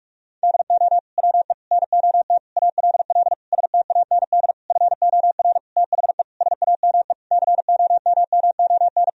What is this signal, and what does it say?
Morse code, heard as a keyed tone, about 46 wpm, 700 Hz. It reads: DO WE NOT ALL STAND FOR THE SAME COMMON
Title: Sentences from Top 1000 Words - Review Part 3 - 45wpm